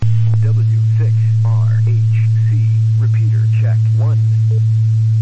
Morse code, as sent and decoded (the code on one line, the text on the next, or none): none